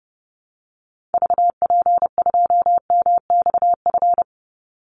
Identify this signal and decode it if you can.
Morse code, heard as a keyed tone, about 29 wpm, 700 Hz. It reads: VP2MXF
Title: Call Signs Repeated 3x - 30wpm